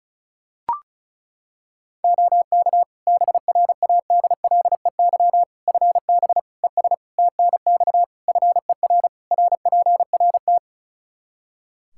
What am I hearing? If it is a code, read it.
Morse code, heard as a keyed tone, about 34 wpm, 700 Hz. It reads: OK BRADLEY FB ES TNX FER RPRT
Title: Practice QSO - Exchange 2 - Info - 35wpm